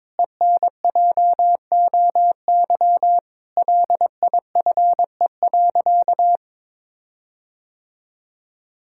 Morse code, heard as a keyed tone, about 21 wpm, 700 Hz. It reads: ENJOY LIFE.
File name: practice / 22wpm / Sentences with 8-Word Phrases - 22wpm